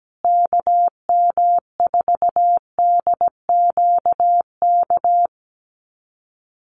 Morse code, heard as a keyed tone, about 17 wpm, 700 Hz. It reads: KM4DQK
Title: USA Call Signs Repeated 2x 17wpm